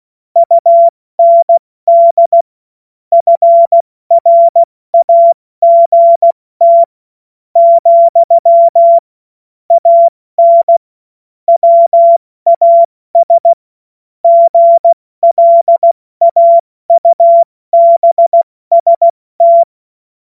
Morse code, heard as a keyed tone, about 16 wpm, 680 Hz. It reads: UND FRAGT , AN WAS GLAUBST
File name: Morsekurs / Witze / Witz 09. Februar 2021, 70 F80